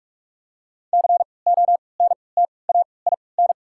Morse code, heard as a keyed tone, about 47 wpm, 690 Hz. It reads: C O N T A I N
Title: 3x Character Spacing - Top 500 Words - 45wpm